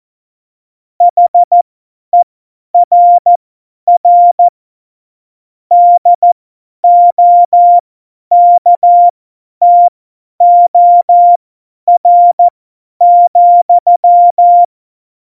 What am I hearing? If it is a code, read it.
Morse code, heard as a keyed tone, about 14 wpm, spaced as for 10 wpm, 700 Hz. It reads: HERR DOKTOR,
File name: Morsekurs / Witze / Witz 22. August 17, Tempo 50 F70